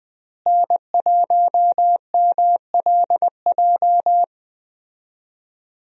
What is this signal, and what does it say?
Morse code, heard as a keyed tone, about 21 wpm, 700 Hz. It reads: N1MLJ
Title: USA Call Signs - 5 Characters - Repeated 3x - 20wpm